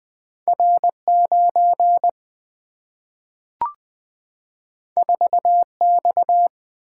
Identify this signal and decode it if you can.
Morse code, heard as a keyed tone, about 20 wpm, 700 Hz. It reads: R9 4X
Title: Single-Letter Single-Number 20wpm